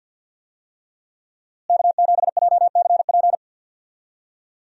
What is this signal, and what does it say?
Morse code, heard as a keyed tone, about 51 wpm, 700 Hz. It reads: K7JCP
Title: USA Call Signs - 5 Characters - Repeated 3x - 50wpm